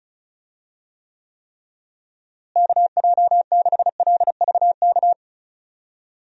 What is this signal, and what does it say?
Morse code, heard as a keyed tone, about 36 wpm, 700 Hz. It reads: KJ6LVK
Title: USA Call Signs - 6 Characters - Repeated 3x - 35wpm